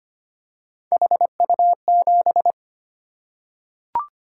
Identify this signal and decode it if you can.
Morse code, heard as a keyed tone, about 25 wpm, 700 Hz. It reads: HU7
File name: Three Letter-Number Combinations - 25wpm